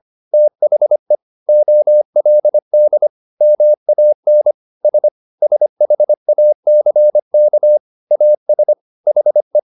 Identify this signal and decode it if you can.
Morse code, heard as a keyed tone, about 25 wpm, 600 Hz. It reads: THE OLD MAN S SHACK AS HE